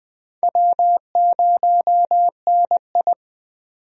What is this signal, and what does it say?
Morse code, heard as a keyed tone, about 20 wpm, 700 Hz. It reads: W0NI
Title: USA Call Signs - 4 Characters - 20wpm